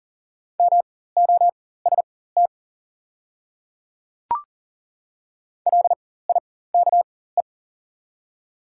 Morse code, heard as a keyed tone, about 40 wpm, spaced as for 19 wpm, 690 Hz. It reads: MOST LIKE
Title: 4x Character Spacing - Top 100 Words - 40wpm